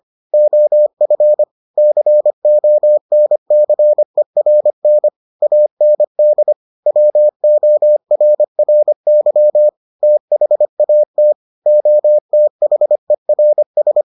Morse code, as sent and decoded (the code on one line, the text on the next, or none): --- ..-. / -.-. --- -. -.-. .-. -. / .- -. -.. / .-- --- .-. .-. -.-- / - .... .- - / --- - .... .-. ...
OF CONCERN AND WORRY THAT OTHERS